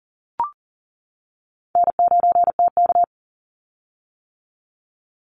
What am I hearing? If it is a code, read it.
Morse code, heard as a keyed tone, about 40 wpm, 700 Hz. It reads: N9TK
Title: Easy Call Signs - 40wpm